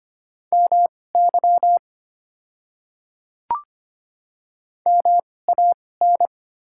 Morse code, heard as a keyed tone, about 25 wpm, spaced as for 18 wpm, 700 Hz. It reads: MY MAN
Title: 2x Character Spacing - Top 200 Words - 25wpm